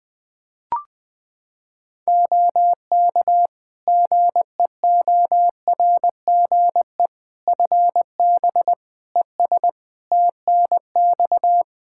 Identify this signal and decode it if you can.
Morse code, heard as a keyed tone, about 21 wpm, 700 Hz. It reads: OK GEORGE FB ES TNX
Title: K1USN SST QSO Practice 20wpm